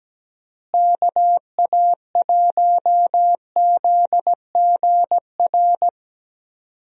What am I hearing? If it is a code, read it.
Morse code, heard as a keyed tone, about 17 wpm, 700 Hz. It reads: KA1ZGR
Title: USA Call Signs - 6 Characters - Repeated 3x - 17wpm